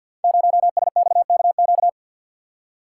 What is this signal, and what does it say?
Morse code, heard as a keyed tone, about 49 wpm, 700 Hz. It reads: 0SXKQ
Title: Code Groups - 5 Letter:Numbers - 50wpm